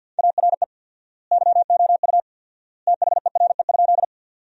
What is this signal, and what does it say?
Morse code, heard as a keyed tone, about 50 wpm, 700 Hz. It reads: ARE YOU THERE?
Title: 4x Word-Spacing - Sentences from Top 100 Words - 50wpm